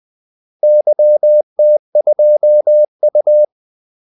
Morse code, heard as a keyed tone, about 20 wpm, 600 Hz. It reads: YT2U
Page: Kaldesignaler 20 wpm